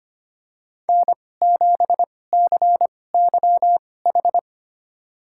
Morse code, heard as a keyed tone, about 25 wpm, spaced as for 18 wpm, 700 Hz. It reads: N7CYH
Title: USA Call Signs - 5 Characters - 2x Character Spacing - 25wpm